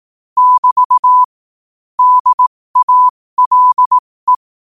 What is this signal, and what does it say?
Morse code, heard as a keyed tone, about 18 wpm, spaced as for 15 wpm, 1000 Hz. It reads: <BT> DALE
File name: qotdfiles / 15wpm